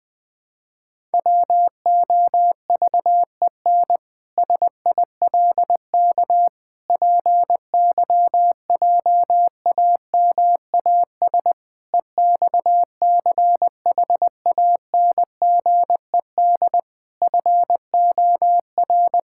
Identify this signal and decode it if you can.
Morse code, heard as a keyed tone, about 21 wpm, 700 Hz. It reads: WOVEN SILK PYJAMAS EXCHANGED FOR
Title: Pangram Challenge v2 - 20wpm